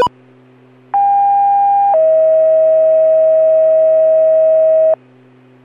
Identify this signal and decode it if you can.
Two-tone page, heard as a one-second tone, then a three-second tone, about 800 then 640 Hz. It is heard from 0.9 to 4.9 s.